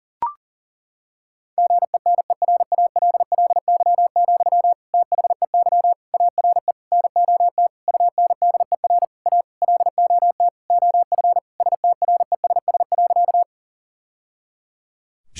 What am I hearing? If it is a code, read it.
Morse code, heard as a keyed tone, about 40 wpm, 700 Hz. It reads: GENERALLY, THEY ARE NOT UNDER A LOT OF STRESS.
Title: Sentences from Top 1000 Words - Review Part 6 - 40wpm